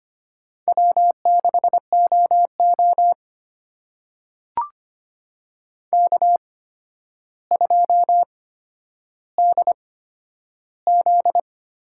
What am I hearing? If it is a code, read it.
Morse code, heard as a keyed tone, about 25 wpm, spaced as for 11 wpm, 700 Hz. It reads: W6OO K2DZ
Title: USA Call Signs - 4 Characters - 8x Character Spacing - 25wpm